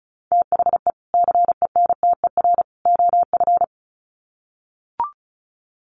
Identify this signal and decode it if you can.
Morse code, heard as a keyed tone, about 34 wpm, 700 Hz. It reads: THE CENTER OF
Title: Sets of 3 Words 35wpm